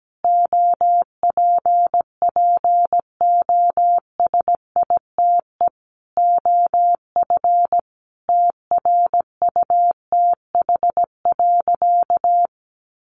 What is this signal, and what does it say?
Morse code, heard as a keyed tone, about 17 wpm, 700 Hz. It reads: OPPOSITE OF TRUTH.
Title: Sentences from Top 1300 Words 17wpm